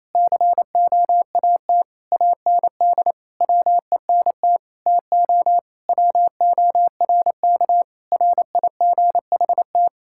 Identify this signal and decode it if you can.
Morse code, heard as a keyed tone, about 28 wpm, 700 Hz. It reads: COAT AND WENT TO WORK RIGHT
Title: Sentences from Top 1600 Words 28wpm